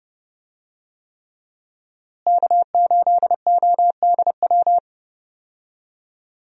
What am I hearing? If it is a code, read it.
Morse code, heard as a keyed tone, about 30 wpm, 700 Hz. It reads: K8ODW
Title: USA Call Signs - 5 Characters - Repeated 3x - 30wpm